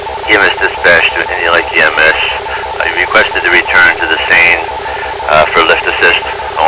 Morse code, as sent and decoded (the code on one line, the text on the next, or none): none